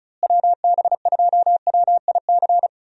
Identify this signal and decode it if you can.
Morse code, heard as a keyed tone, about 36 wpm, 700 Hz. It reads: WB2WIC
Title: USA Call Signs - 6 Characters - Repeated 3x - 35wpm